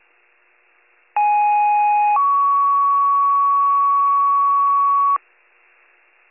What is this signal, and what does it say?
Two-tone page, heard as a one-second tone, then a three-second tone, about 830 then 1120 Hz. It is heard from 1.2 to 5.2 s.